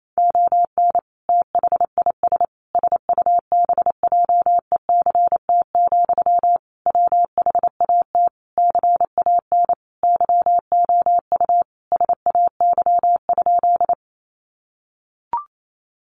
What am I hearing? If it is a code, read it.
Morse code, heard as a keyed tone, about 28 wpm, 700 Hz. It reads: ON THIS SUBJECT, WHAT CAN YOU SAY?
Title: Sentences from Top 800 Words 28wpm